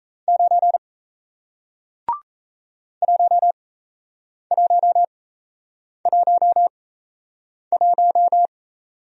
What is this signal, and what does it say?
Morse code, heard as a keyed tone, about 38 wpm, spaced as for 8 wpm, 700 Hz. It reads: 9 1111